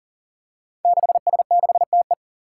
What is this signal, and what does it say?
Morse code, heard as a keyed tone, about 42 wpm, 700 Hz. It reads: 6S6TE
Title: Five Challenging Letter-Number Combinations 40wpm